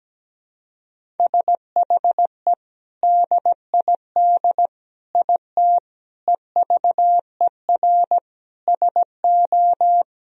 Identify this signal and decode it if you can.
Morse code, heard as a keyed tone, about 17 wpm, 700 Hz. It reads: SHE DID IT EVER SO
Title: Sentences from Top 400 Words and Abbreviations 17wpm